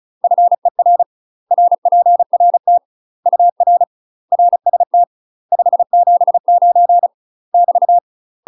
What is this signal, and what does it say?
Morse code, heard as a keyed tone, about 34 wpm, 700 Hz. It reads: FER RPRT UR RST 579 <BT>
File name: UK MASTER QSOs TRAINING / 35WPM / UK MASTER QSOs 0004